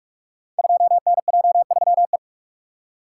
Morse code, heard as a keyed tone, about 44 wpm, 700 Hz. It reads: 2NJ3E
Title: Code Groups - 5 Letter:Numbers - 45wpm